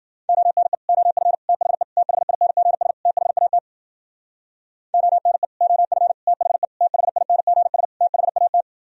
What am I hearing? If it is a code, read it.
Morse code, heard as a keyed tone, about 53 wpm, 700 Hz. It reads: ONE OF THE THINGS THAT ONE OF THE THINGS THAT